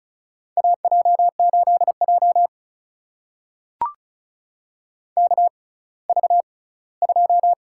Morse code, heard as a keyed tone, about 36 wpm, spaced as for 20 wpm, 700 Hz. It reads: AJ8J KV2
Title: USA Call Signs - 4 Characters - 6x Character Spacing - 35wpm